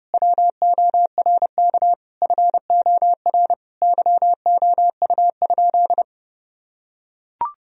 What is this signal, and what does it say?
Morse code, heard as a keyed tone, about 32 wpm, 700 Hz. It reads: WORK FOR YOU?
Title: Sentences from Top 1000 Words - Review Part 4 - 30wpm